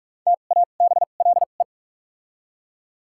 Morse code, heard as a keyed tone, about 41 wpm, spaced as for 31 wpm, 700 Hz. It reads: TABLE
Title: 2x Character Spacing - Top 500 Words - 45wpm